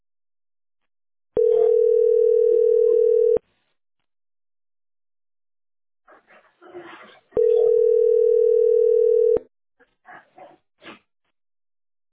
Ringback tone (North America), 440 + 480 Hz.